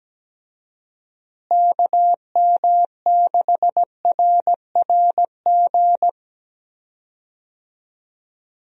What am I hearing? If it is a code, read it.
Morse code, heard as a keyed tone, about 17 wpm, 700 Hz. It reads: KM6RRG
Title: USA Call Signs - 6 Characters - Repeated 2x - 17wpm